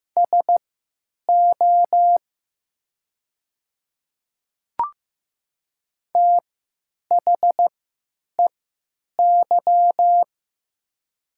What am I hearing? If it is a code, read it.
Morse code, heard as a keyed tone, about 15 wpm, spaced as for 9 wpm, 700 Hz. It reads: SO THEY
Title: 3x Character Spacing - Top 100 Words - 15wpm